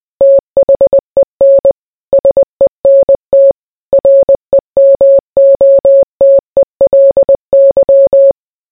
Morse code, heard as a keyed tone, about 20 wpm, 550 Hz. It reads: THEN SENT REMOTELY